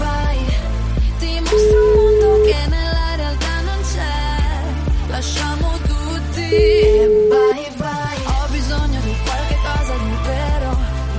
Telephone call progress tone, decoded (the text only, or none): ringback tone